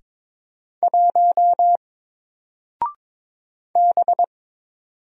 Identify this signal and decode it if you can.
Morse code, heard as a keyed tone, about 22 wpm, 700 Hz. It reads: 1 B
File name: Single-Letter-Number-22wpm